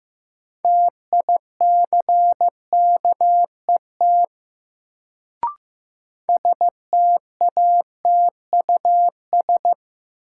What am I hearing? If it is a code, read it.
Morse code, heard as a keyed tone, about 15 wpm, 700 Hz. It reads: TICKET STATUS
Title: Sentences from Top 1400 Words - New Word Drill - 15wpm